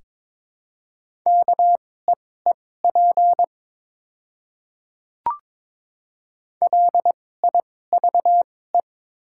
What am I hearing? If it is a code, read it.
Morse code, heard as a keyed tone, about 23 wpm, spaced as for 16 wpm, 700 Hz. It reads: KEEP LIVE
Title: 2x Character Spacing - Top 300 Words - 22wpm